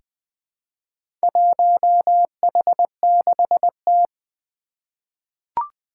Morse code, heard as a keyed tone, about 20 wpm, 700 Hz. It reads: 1H6T
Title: Four Challenging Letter-Number Combinations 20wpm